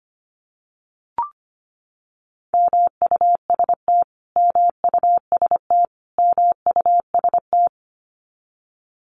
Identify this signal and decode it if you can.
Morse code, heard as a keyed tone, about 25 wpm, 700 Hz. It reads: MUST MUST MUST
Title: Top 500 Words Repeated 25wpm